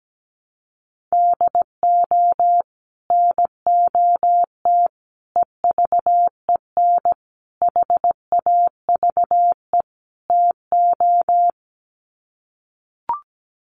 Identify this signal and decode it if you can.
Morse code, heard as a keyed tone, about 17 wpm, 700 Hz. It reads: DO NOT EVEN HAVE TO
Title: Sets of 5 Words Encore2 17wpm